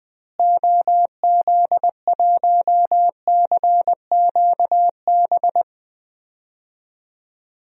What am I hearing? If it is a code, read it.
Morse code, heard as a keyed tone, about 20 wpm, 700 Hz. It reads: OZ1CQB
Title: DXCC-100 Call Signs - 6 Characters - Repeated 3x - 20wpm